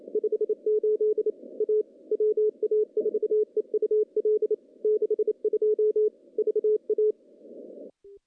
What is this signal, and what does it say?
Morse code, heard as a keyed tone, about 28 wpm, 430 Hz. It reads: 58 A WA4EUL 62 VA
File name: contests / CW-SS-k5tr- -0100